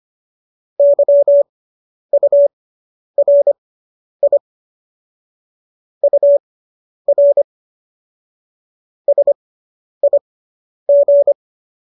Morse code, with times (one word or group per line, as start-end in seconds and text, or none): 0.8-4.4 YURI
6.0-7.4 UR
9.1-11.3 SIG